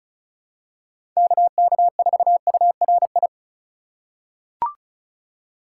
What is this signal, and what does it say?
Morse code, heard as a keyed tone, about 34 wpm, 700 Hz. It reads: KK4URI
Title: USA Call Signs - 4x Character Spacing - 35wpm